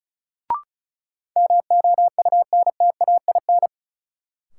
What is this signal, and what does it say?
Morse code, heard as a keyed tone, about 36 wpm, 700 Hz. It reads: MOUNTAIN
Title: Sentences from Top 1000 Words - Review Part 1 - New Word Drill - 35wpm